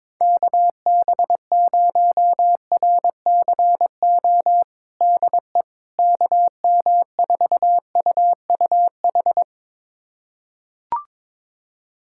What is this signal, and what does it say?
Morse code, heard as a keyed tone, about 22 wpm, 700 Hz. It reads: KB0RCO DE KM4UUH
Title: USA Callsign DE Callsign - 6 Characters - 22wpm